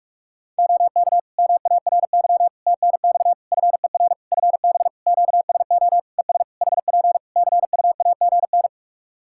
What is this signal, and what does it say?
Morse code, heard as a keyed tone, about 46 wpm, 700 Hz. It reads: OK MARY TNX FER FB QSO ES HP CUAGN
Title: Practice QSO - Exchange 4 - Ending - 45wpm